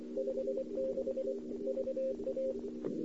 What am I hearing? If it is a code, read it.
Morse code, heard as a keyed tone, about 24 wpm, 530 Hz. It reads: 56 VA
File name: CW / 2008-ss-cw-k5tr-op-n5rz -02:00